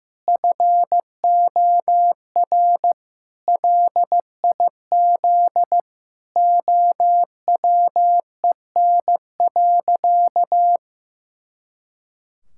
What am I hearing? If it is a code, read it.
Morse code, heard as a keyed tone, about 15 wpm, 700 Hz. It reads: FOR LIZ OWEN.
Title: Pangram Challenge v2 - 15wpm